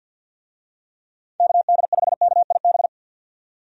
Morse code, heard as a keyed tone, about 51 wpm, 700 Hz. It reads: KD5KIB
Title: USA Call Signs - 6 Characters - 50wpm